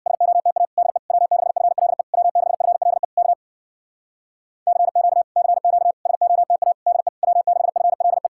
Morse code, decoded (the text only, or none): SOTA DE W6FLE W6FLE K CQ CQ SOTA DE W6FLE